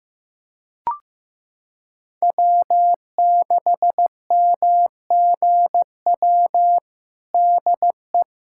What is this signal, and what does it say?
Morse code, heard as a keyed tone, about 15 wpm, 700 Hz. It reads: W6MGW DE
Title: USA Callsign DE Callsign 15wpm